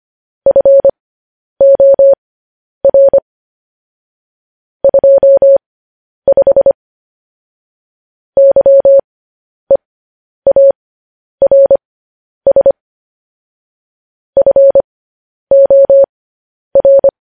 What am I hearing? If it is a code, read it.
Morse code, heard as a keyed tone, about 25 wpm, spaced as for 10 wpm, 550 Hz. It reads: FOR 25 YEARS FOR